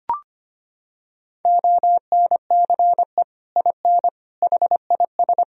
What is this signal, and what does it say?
Morse code, heard as a keyed tone, about 25 wpm, 700 Hz. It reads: ONCE IN HIS
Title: Sentences from Top 1000 Words - Review Part 2 - 25wpm